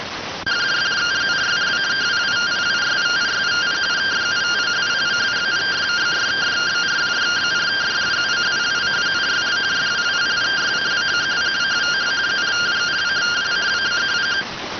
Two-tone teleprinter signal, about 170 Hz shift.